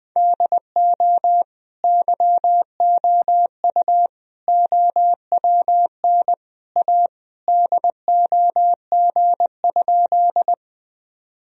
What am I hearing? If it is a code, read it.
Morse code, heard as a keyed tone, about 20 wpm, 700 Hz. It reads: DO YOU OWN A DOG?
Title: Sentences from Top 1000 Words - Review Part 2 - 20wpm